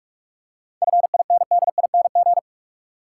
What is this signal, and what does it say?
Morse code, heard as a keyed tone, about 43 wpm, 700 Hz. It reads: FINDING
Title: Seven Letter Words - 45wpm